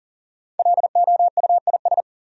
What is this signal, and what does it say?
Morse code, heard as a keyed tone, about 40 wpm, 700 Hz. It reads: LOUIS
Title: USA Common Names - 40wpm